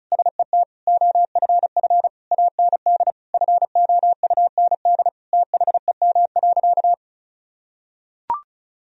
Morse code, heard as a keyed tone, about 36 wpm, 700 Hz. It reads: SET OFF AND FOUND THEM.
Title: Sentences from Top 1000 Words - Review Part 1 - 35wpm